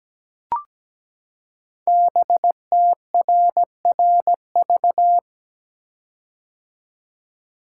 Morse code, heard as a keyed tone, about 17 wpm, 700 Hz. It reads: BTRRV
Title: Code Groups - 5 Letters - 17wpm